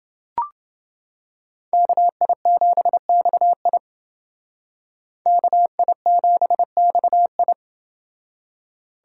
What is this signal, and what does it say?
Morse code, heard as a keyed tone, about 28 wpm, 700 Hz. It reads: KI7XI KI7XI